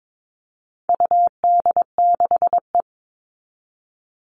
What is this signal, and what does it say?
Morse code, heard as a keyed tone, about 22 wpm, 700 Hz. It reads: UD6E
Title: All Cut Numbers - 4-Digit 22wpm